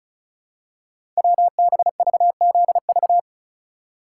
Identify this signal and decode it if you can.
Morse code, heard as a keyed tone, about 36 wpm, 700 Hz. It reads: WBVZV
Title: Code Groups - 5 Letters - 35wpm